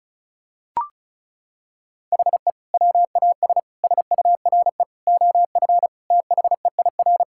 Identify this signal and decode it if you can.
Morse code, heard as a keyed tone, about 36 wpm, 700 Hz. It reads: HE WAS SURE OF THEIR
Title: Sentences from Top 1000 Words - Review Part 6 - 35wpm